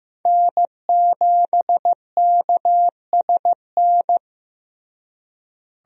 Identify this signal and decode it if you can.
Morse code, heard as a keyed tone, about 15 wpm, 700 Hz. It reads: N7KSN
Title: USA Call Signs - 5 Characters - Repeated 2x - 15wpm